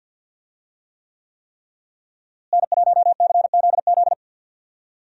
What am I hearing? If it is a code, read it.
Morse code, heard as a keyed tone, about 52 wpm, 700 Hz. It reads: N1CZZ
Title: USA Call Signs - 5 Characters - Repeated 3x - 50wpm